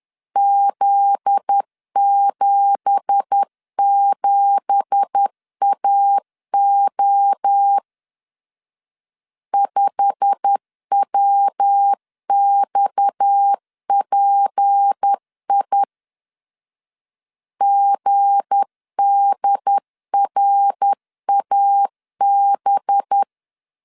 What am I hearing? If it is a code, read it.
Morse code, heard as a keyed tone, about 10 wpm, 800 Hz. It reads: Z77AO 5WXPI GDRAB
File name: LEZIONI / 9 2867 10wpm 2733